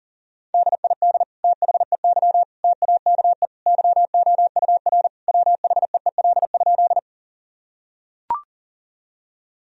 Morse code, heard as a keyed tone, about 37 wpm, 700 Hz. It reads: DID THEY TAKE YOUR WHEEL?
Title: Sentences from Top 1000 Words - Review Part 3 - 40wpm